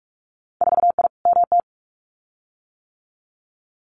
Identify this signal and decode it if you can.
Morse code, heard as a keyed tone, about 46 wpm, 690 Hz. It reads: 4I MT
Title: Winter Field Day Exchanges Encore 45wpm